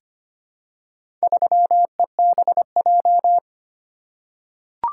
Morse code, heard as a keyed tone, about 25 wpm, 700 Hz. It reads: 3EBJ
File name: practice / 25wpm / Four Letter-Number Combinations - 25wpm